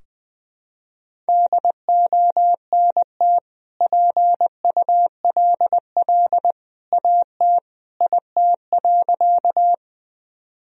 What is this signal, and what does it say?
Morse code, heard as a keyed tone, about 20 wpm, 700 Hz. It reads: DONT PULL AT IT.
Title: Sentences from Top 1000 Words - Review Part 2 - 20wpm